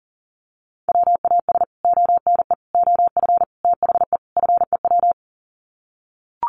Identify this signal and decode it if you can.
Morse code, heard as a keyed tone, about 40 wpm, 700 Hz. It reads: WAS ONE OF THE FEW
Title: Sets of 5 Words Encore 40wpm